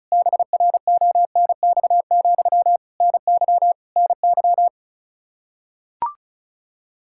Morse code, heard as a keyed tone, about 36 wpm, 700 Hz. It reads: BRONX, NY NY
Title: Practice QSO - Exchange 1 - Beginning - 35wpm